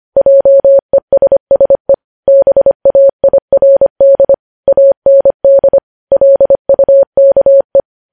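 Morse code, heard as a keyed tone, about 25 wpm, 550 Hz. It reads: JESSE BAIRD AND LUKE
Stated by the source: Morse code news headlines sent at 25 wpm.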